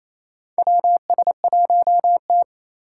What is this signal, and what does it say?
Morse code, heard as a keyed tone, about 28 wpm, 700 Hz. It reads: WS1T
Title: USA Call Signs - 3x Character Spacing - 28wpm